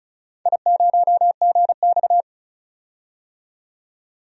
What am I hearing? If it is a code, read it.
Morse code, heard as a keyed tone, about 33 wpm, 700 Hz. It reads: I0GX
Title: DXCC-100 Call Signs - 4 Characters - 35wpm